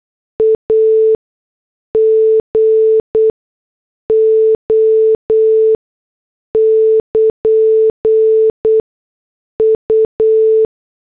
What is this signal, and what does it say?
Morse code, heard as a keyed tone, about 8 wpm, spaced as for 6 wpm, 440 Hz. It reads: AGO <KN> U
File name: morse / headlines